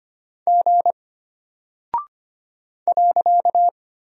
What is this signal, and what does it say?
Morse code, heard as a keyed tone, about 25 wpm, 700 Hz. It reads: G .
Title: Single Character (Letter:Number:Punctuation:Prosign) Alt 25wpm